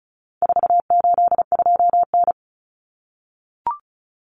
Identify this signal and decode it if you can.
Morse code, heard as a keyed tone, about 34 wpm, 700 Hz. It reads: 482N
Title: Common Cut Numbers - 4-Digit 35wpm